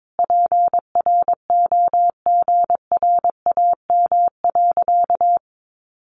Morse code, heard as a keyed tone, about 22 wpm, 700 Hz. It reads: PROGRAM.
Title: Sentences from Top 1500 Words 22wpm